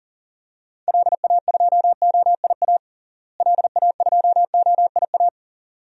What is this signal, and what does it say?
Morse code, heard as a keyed tone, about 40 wpm, 700 Hz. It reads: LA2OIA LA2OIA
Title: DXCC-100 Call Signs - 6 Characters - Repeated 2x - 40wpm